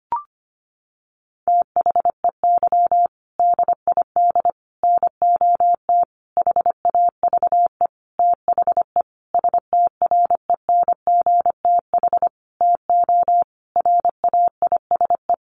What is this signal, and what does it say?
Morse code, heard as a keyed tone, about 25 wpm, 700 Hz. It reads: THEY DID NOT HAVE THE STRENGTH TO RAISE
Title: Sentences from Top 1400 Words 25wpm